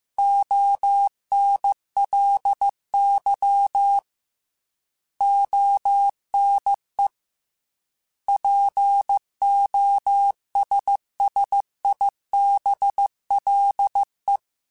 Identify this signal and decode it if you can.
Morse code, heard as a keyed tone, about 15 wpm, 800 Hz. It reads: ONLY ONE POSSIBLE
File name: Technical Subjects / Technical Subjects-mp3s / AND Gate @13